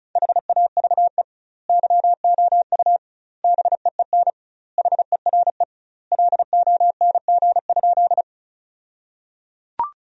Morse code, heard as a keyed tone, about 36 wpm, 700 Hz. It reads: HAVE YOU BEEN HERE LONG?
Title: 2x Word-Spacing - Sentences from Top 200 Words - 35wpm